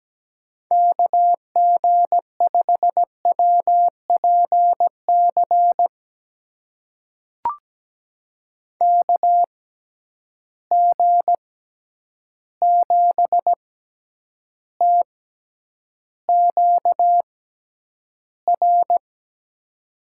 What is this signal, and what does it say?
Morse code, heard as a keyed tone, about 17 wpm, spaced as for 9 wpm, 700 Hz. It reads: KG5WPC KG7TQR
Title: USA Call Signs - 6 Characters - 6x Character Spacing - 17wpm